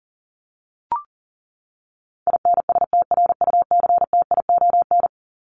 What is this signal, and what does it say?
Morse code, heard as a keyed tone, about 40 wpm, 700 Hz. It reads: INSTRUCTION
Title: Top 1751-2000 Words 40wpm